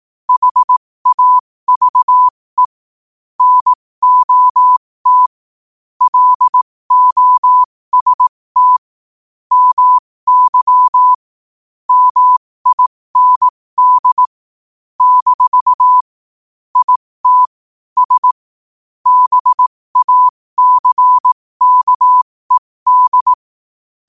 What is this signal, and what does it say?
Morse code, heard as a keyed tone, about 18 wpm, spaced as for 15 wpm, 1000 Hz. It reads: HAVE NOT LOST MY MIND - IT S BACKED
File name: qotdfiles / 15wpm